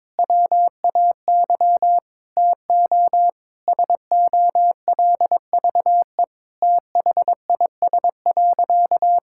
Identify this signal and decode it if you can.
Morse code, heard as a keyed tone, about 22 wpm, 700 Hz. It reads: WAY TO SOLVE THIS.
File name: Sentences with 7-Word Phrases - 22wpm